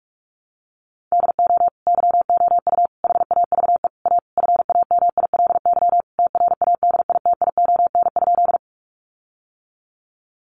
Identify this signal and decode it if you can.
Morse code, heard as a keyed tone, about 44 wpm, 700 Hz. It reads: DO YOU HAVE A FAMILY TRADITION?
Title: Sentences from Top 1500 Words 45wpm